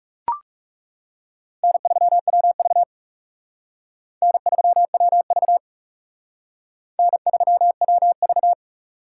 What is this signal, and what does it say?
Morse code, heard as a keyed tone, about 38 wpm, 700 Hz. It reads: N3WV N3WV N3WV